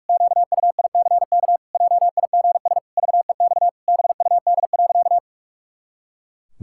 Morse code, heard as a keyed tone, about 48 wpm, 700 Hz. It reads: QUICK JIGS VEX BUD.
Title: Pangram Challenge v2 - 45wpm